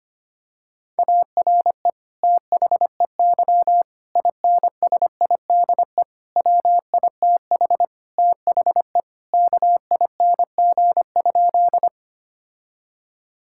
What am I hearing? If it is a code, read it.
Morse code, heard as a keyed tone, about 25 wpm, 700 Hz. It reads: ARE THEY INSIDE WITH THE KING?
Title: Sentences from Top 1000 Words - Review Part 6 - 25wpm